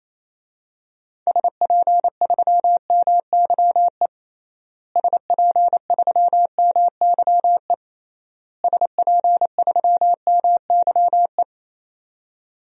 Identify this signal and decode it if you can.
Morse code, heard as a keyed tone, about 28 wpm, 700 Hz. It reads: SP3MYE SP3MYE SP3MYE
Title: DXCC-100 Call Signs - 6 Characters - Repeated 3x - 28wpm